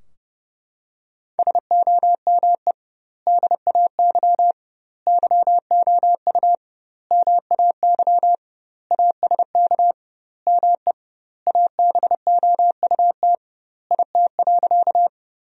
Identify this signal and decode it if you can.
Morse code, heard as a keyed tone, about 30 wpm, 700 Hz. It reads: SOME DAY YOU MAY ASK ME ABOUT IT.
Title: 2x Word-Spacing - Sentences from Top 200 Words - 30wpm